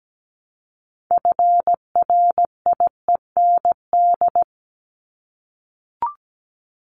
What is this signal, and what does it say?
Morse code, heard as a keyed tone, about 17 wpm, 700 Hz. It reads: FRIEND
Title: Top 500 Words 17wpm